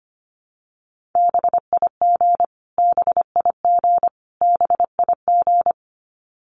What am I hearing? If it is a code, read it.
Morse code, heard as a keyed tone, about 25 wpm, 700 Hz. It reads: BIG BIG BIG
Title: Top 200 Words Repeated 25wpm